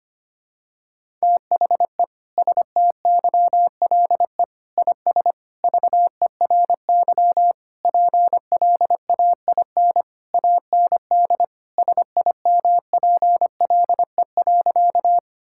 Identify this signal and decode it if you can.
Morse code, heard as a keyed tone, about 25 wpm, 700 Hz. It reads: THE STYLE IS VERY PLAIN AND SIMPLE.
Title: Sentences from Top 1000 Words - Review Part 5 - 25wpm